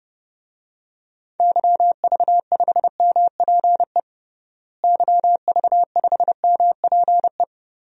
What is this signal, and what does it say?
Morse code, heard as a keyed tone, about 30 wpm, 700 Hz. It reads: YV5MPE YV5MPE
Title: DXCC-100 Call Signs - 6 Characters - Repeated 3x - 30wpm